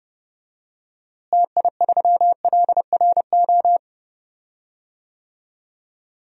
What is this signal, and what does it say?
Morse code, heard as a keyed tone, about 31 wpm, 700 Hz. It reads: TI3LRO
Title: DXCC-100 Call Signs - 6 Characters - Repeated 2x - 30wpm